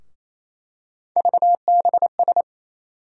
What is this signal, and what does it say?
Morse code, heard as a keyed tone, about 27 wpm, 700 Hz. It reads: VBS
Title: USA Call Sign Postfixes - 3 Letters 28wpm